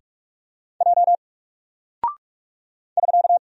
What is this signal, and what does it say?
Morse code, heard as a keyed tone, about 44 wpm, 700 Hz. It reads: J <SK>
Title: Single Character (Letter:Number:Punctuation:Prosign) Alt 45wpm